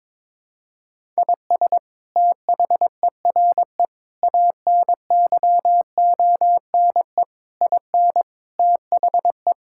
Morse code, heard as a keyed tone, about 21 wpm, 700 Hz. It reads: IS THERE ANYONE IN THE
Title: Sentences from Top 1000 Words - Review Part 5 - 22wpm